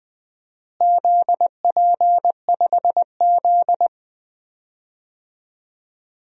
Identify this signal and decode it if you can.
Morse code, heard as a keyed tone, about 20 wpm, 700 Hz. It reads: ZP5Z
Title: DXCC-100 Call Signs - 4 Characters - Repeated 2x - 20wpm